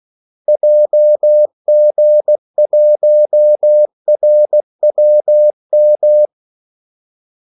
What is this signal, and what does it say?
Morse code, heard as a keyed tone, about 16 wpm, 600 Hz. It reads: JG1RWM
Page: Kaldesignaler 16 wpm